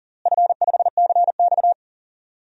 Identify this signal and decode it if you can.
Morse code, heard as a keyed tone, about 38 wpm, 700 Hz. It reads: F5CX
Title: DXCC-100 Call Signs - 4 Characters - Repeated 2x - 40wpm